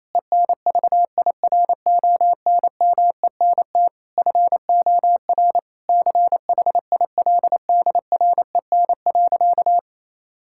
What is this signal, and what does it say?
Morse code, heard as a keyed tone, about 27 wpm, 700 Hz. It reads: ENVIRONMENT FOR CHILDREN.
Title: Sentences from Top 1000 Words - Review Part 5 - 28wpm